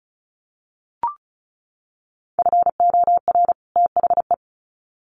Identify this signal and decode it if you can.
Morse code, heard as a keyed tone, about 34 wpm, 700 Hz. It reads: FOR THE
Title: Sets of 2 Words 35wpm